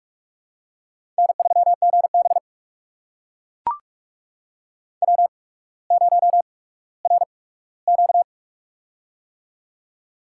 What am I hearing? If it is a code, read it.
Morse code, heard as a keyed tone, about 44 wpm, spaced as for 19 wpm, 690 Hz. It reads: N3GB W0RQ
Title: USA Call Signs - 4 Characters - 8x Character Spacing - 45wpm